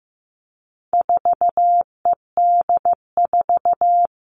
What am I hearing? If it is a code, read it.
Morse code, heard as a keyed tone, about 15 wpm, 700 Hz. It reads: 4ED4
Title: All Cut Numbers - 4-Digit 15wpm